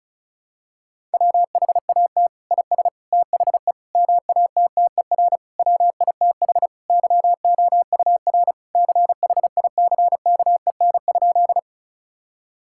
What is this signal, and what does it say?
Morse code, heard as a keyed tone, about 36 wpm, 700 Hz. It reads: WHAT IS THE MATTER WITH YOUR CHICKEN?
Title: Sentences from Top 1000 Words - Review Part 5 - 35wpm